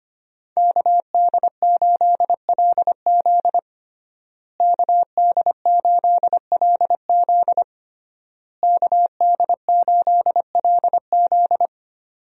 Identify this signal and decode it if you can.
Morse code, heard as a keyed tone, about 25 wpm, 700 Hz. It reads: KD8LZ KD8LZ KD8LZ
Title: USA Call Signs - 5 Characters - Repeated 3x - 25wpm